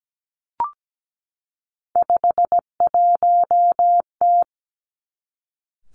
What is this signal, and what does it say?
Morse code, heard as a keyed tone, about 17 wpm, 700 Hz. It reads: 51T
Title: Three Challenging Letter-Number Combinations 17wpm